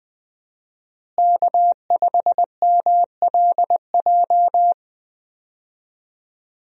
Morse code, heard as a keyed tone, about 20 wpm, 700 Hz. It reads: K5MLJ
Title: USA Call Signs - 5 Characters - Repeated 2x - 20wpm